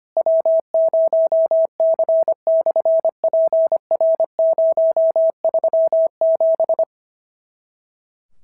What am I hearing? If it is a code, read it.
Morse code, heard as a keyed tone, about 25 wpm, 650 Hz. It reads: W0C/PR037
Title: SOTA QSO Practice 25wpm